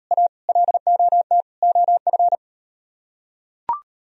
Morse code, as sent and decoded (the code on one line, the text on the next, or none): .- / .-.. --- - / --- ..-.
A LOT OF